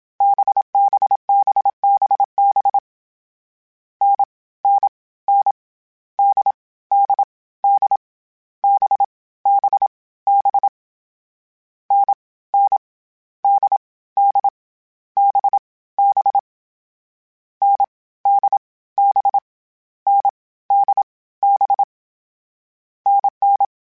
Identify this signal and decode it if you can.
Morse code, heard as a keyed tone, about 27 wpm, 800 Hz. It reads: BBBBB N N N D D D B B B N N D D B B N D B N D B NN
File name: Variable Speed / Variable Speed-mp3s / GZ GZ7 sp @25